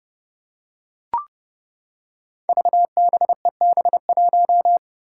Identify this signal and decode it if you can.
Morse code, heard as a keyed tone, about 30 wpm, 700 Hz. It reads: VBEB1
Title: Five Challenging Letter-Number Combinations 30wpm